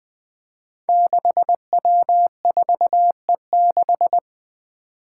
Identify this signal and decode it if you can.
Morse code, heard as a keyed tone, about 20 wpm, 700 Hz. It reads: 6W4E6
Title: Five Challenging Letter-Number Combinations 20wpm